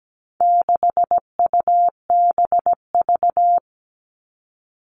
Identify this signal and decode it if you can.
Morse code, heard as a keyed tone, about 17 wpm, 700 Hz. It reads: 6UBV
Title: All Cut Numbers - 4-Digit 17wpm